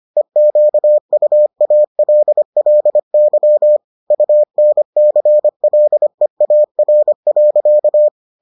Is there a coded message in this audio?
Yes: Morse code from 0.2 to 8.1 s, heard as a keyed tone, about 25 wpm, 600 Hz. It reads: EQUALLY UNCLEAR.